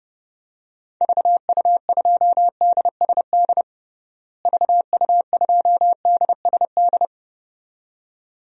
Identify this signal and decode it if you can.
Morse code, heard as a keyed tone, about 32 wpm, 700 Hz. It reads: VU2DSD VU2DSD
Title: DXCC-100 Call Signs - 6 Characters - Repeated 2x - 30wpm